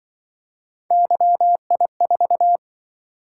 Morse code, heard as a keyed tone, about 25 wpm, 700 Hz. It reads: YI4
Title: Speed-Racing - Three Letter-Number Combinations - 20wpm